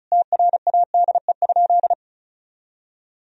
Morse code, heard as a keyed tone, about 34 wpm, 700 Hz. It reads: TRADE?
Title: Sentences from Top 1000 Words - Review Part 4 - 35wpm